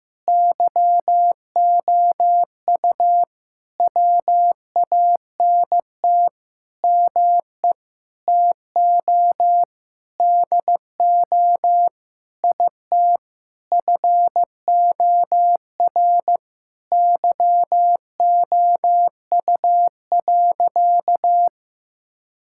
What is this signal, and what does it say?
Morse code, heard as a keyed tone, about 15 wpm, 700 Hz. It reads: YOU WANT ME TO DO IT FOR YOU.
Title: Sentences from Top 1000 Words - Review Part 6 - 15wpm